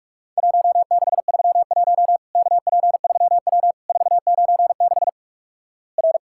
Morse code, heard as a keyed tone, about 44 wpm, 700 Hz. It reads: 1631 KP3W 496 R
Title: Worked All Europe DX Contest - QTC Exchange - 45wpm